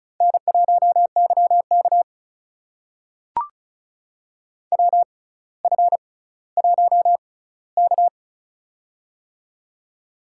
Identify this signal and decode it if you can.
Morse code, heard as a keyed tone, about 36 wpm, spaced as for 18 wpm, 700 Hz. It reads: N1YK WF1K
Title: USA Call Signs - 4 Characters - 6x Character Spacing - 35wpm